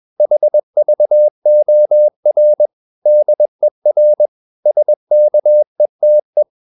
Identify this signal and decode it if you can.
Morse code, heard as a keyed tone, about 21 wpm, 600 Hz. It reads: HVOR DER SKETE
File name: Ferie QRS 21wpm